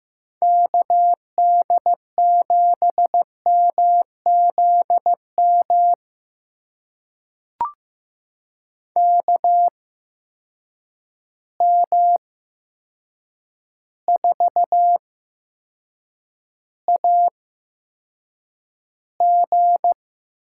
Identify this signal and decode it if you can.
Morse code, heard as a keyed tone, about 15 wpm, spaced as for 7 wpm, 700 Hz. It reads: KD7MZM KM4AG
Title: USA Call Signs - 8x Character Spacing - 15wpm